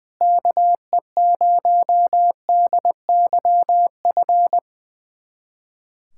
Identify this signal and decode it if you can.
Morse code, heard as a keyed tone, about 20 wpm, 700 Hz. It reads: KE0DYF
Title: USA Callsign DE Callsign 20wpm